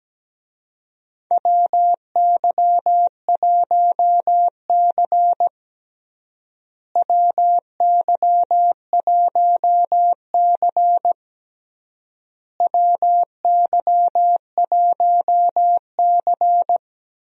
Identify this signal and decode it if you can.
Morse code, heard as a keyed tone, about 17 wpm, 700 Hz. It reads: WY1C WY1C WY1C
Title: USA Call Signs - 4 Characters - Repeated 3x - 17wpm